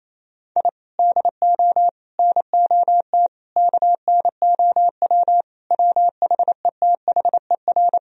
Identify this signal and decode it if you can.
Morse code, heard as a keyed tone, about 28 wpm, 700 Hz. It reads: I DO NOT KNOW WHETHER
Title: Sentences from Top 1000 Words - Review Part 3 - 28wpm